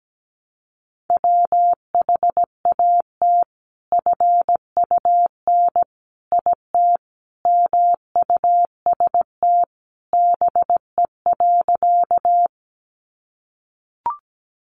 Morse code, heard as a keyed tone, about 17 wpm, 700 Hz. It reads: WHAT FUN IT MUST BE.
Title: Sentences from Top 700 Words 17wpm